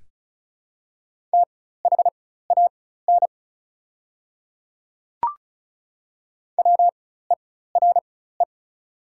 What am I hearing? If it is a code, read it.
Morse code, heard as a keyed tone, about 36 wpm, spaced as for 16 wpm, 700 Hz. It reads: THAN WERE
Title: 4x Character Spacing - Top 300 Words - 35wpm